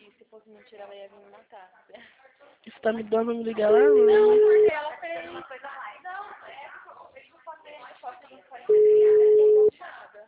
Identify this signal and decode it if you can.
Ringback tone (Europe, 425 Hz), with speech.
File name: out- -2026- - - .11881